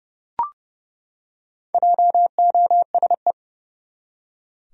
Morse code, heard as a keyed tone, about 32 wpm, 700 Hz. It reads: JOSE
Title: USA Common Names - Max 5 Letters - 30wpm